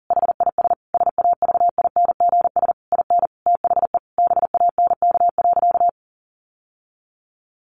Morse code, heard as a keyed tone, about 40 wpm, 700 Hz. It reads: HIS SAVINGS IN THE BANK.
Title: Sentences from Top 1200 Words 40wpm